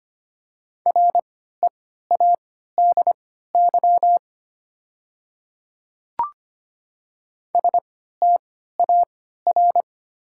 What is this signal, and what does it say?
Morse code, heard as a keyed tone, about 25 wpm, spaced as for 14 wpm, 700 Hz. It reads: READY STAR
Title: 3x Character Spacing - Top 500 Words - 25wpm